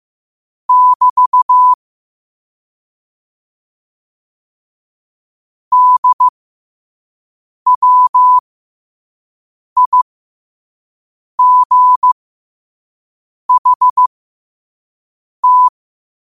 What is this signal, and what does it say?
Morse code, heard as a keyed tone, about 15 wpm, spaced as for 5 wpm, 1000 Hz. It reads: <BT> DWIGHT
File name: qotdfiles / 5wpm